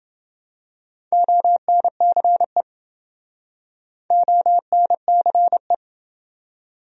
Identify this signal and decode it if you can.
Morse code, heard as a keyed tone, about 30 wpm, 700 Hz. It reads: ONCE ONCE